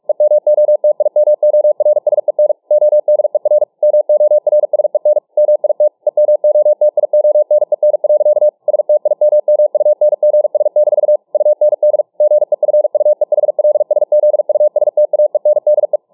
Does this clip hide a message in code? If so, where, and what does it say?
Yes: Morse code from 0.1 to 16.0 s, heard as a keyed tone, about 44 wpm, 600 Hz. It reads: EMOTIMORSEN ODER MORSEN MIT EMOTIONEN. STIMMUNGS- UND GEFUEHLSZUSTAENDE